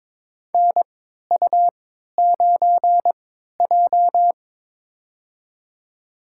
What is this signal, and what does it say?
Morse code, heard as a keyed tone, about 23 wpm, spaced as for 13 wpm, 700 Hz. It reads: NU9J
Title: USA Call Signs - 4 Characters - 3x Character Spacing - 22wpm